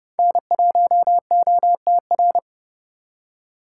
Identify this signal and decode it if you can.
Morse code, heard as a keyed tone, about 38 wpm, 700 Hz. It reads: N1OTR